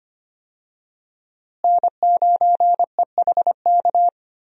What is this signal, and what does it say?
Morse code, heard as a keyed tone, about 25 wpm, 700 Hz. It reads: N9EHK
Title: USA Call Signs - 5 Characters - Repeated 3x - 25wpm